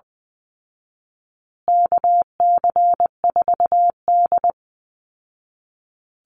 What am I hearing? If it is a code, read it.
Morse code, heard as a keyed tone, about 20 wpm, 700 Hz. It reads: KC4D
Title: Call Signs Repeated 3x - 20wpm